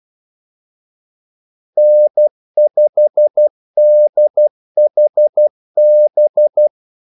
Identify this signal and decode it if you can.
Morse code, heard as a keyed tone, about 12 wpm, 600 Hz. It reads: N5DHB
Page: 2 ekstra ordmellemrum mellem kaldesignalerne